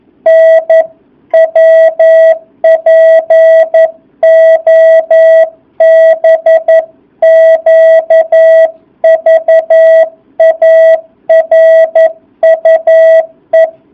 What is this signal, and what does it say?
Morse code, heard as a keyed tone, about 11 wpm, 650 Hz. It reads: NWPOBQVARUE
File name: jh8cbh / 2019cw